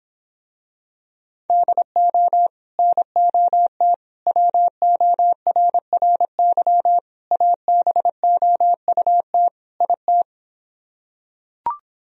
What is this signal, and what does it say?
Morse code, heard as a keyed tone, about 26 wpm, 700 Hz. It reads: DO NOT WORRY ABOUT IT